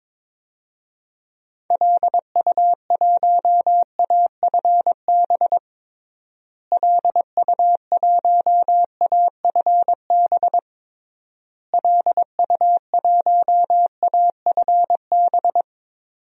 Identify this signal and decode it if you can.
Morse code, heard as a keyed tone, about 22 wpm, 700 Hz. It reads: LU1AFB LU1AFB LU1AFB
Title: DXCC-100 Call Signs - 6 Characters - Repeated 3x - 22wpm